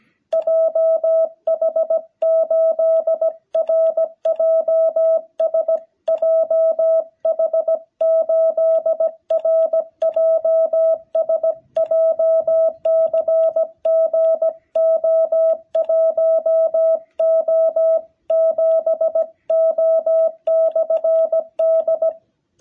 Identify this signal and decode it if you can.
Morse code, heard as a keyed tone, about 17 wpm, 650 Hz. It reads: JH8RJS JH8RJS JCG O1O7O/D
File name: jh8cbh / 2021cw